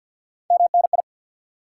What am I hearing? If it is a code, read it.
Morse code, heard as a keyed tone, about 52 wpm, 700 Hz. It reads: MNI
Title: Top 90 QSO Elements 50wpm